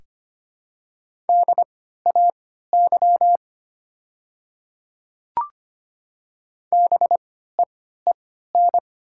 Morse code, heard as a keyed tone, about 25 wpm, spaced as for 14 wpm, 700 Hz. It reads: DAY BEEN
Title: 3x Character Spacing - Top 200 Words - 25wpm